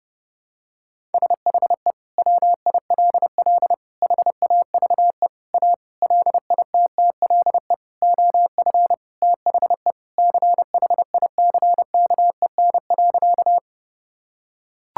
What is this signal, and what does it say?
Morse code, heard as a keyed tone, about 32 wpm, 700 Hz. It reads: SHE WILL HAVE A LITTLE OF THE CHICKEN.
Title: Sentences from Top 1000 Words - Review Part 5 - 30wpm